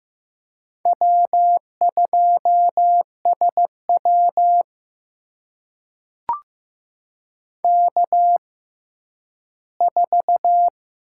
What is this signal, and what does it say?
Morse code, heard as a keyed tone, about 15 wpm, spaced as for 10 wpm, 700 Hz. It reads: W2SW K4
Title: USA Call Signs - 6x Character Spacing - 15wpm